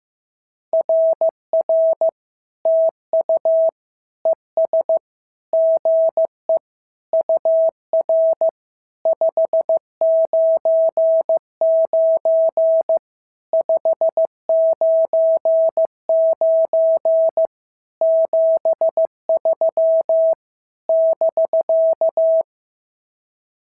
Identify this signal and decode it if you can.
Morse code, heard as a keyed tone, about 15 wpm, 650 Hz. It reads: RR TU ES GE UR 599 599 73 <BK>
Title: SOTA QSO Practice 15wpm